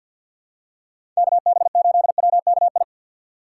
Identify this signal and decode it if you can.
Morse code, heard as a keyed tone, about 50 wpm, 700 Hz. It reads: KB8WKI
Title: USA Call Signs - 6 Characters - 6x Character Spacing - 50wpm